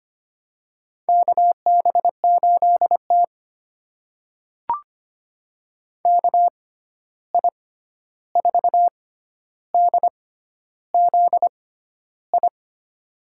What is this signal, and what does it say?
Morse code, heard as a keyed tone, about 25 wpm, spaced as for 11 wpm, 700 Hz. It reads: KB8T KI4DZI
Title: USA Call Signs - 6x Character Spacing - 25wpm